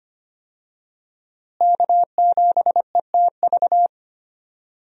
Morse code, heard as a keyed tone, about 25 wpm, 700 Hz. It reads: K7ETV
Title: USA Call Signs - 5 Characters - Repeated 3x - 25wpm